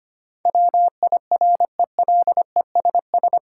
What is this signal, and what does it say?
Morse code, heard as a keyed tone, about 25 wpm, 690 Hz. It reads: WIRELESS
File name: practice / 25wpm / Difficult Words with Hints - 25wpm